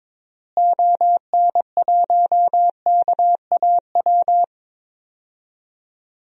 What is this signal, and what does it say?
Morse code, heard as a keyed tone, about 22 wpm, 700 Hz. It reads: ON1KAW
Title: DXCC-100 Call Signs - 6 Characters - Repeated 3x - 22wpm